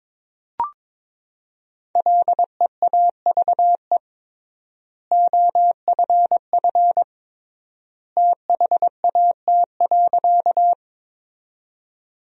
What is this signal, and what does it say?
Morse code, heard as a keyed tone, about 22 wpm, 700 Hz. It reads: LEAVE OFF THAT.
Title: 3x Word-Spacing - Sentences from Top 400 Words - 22wpm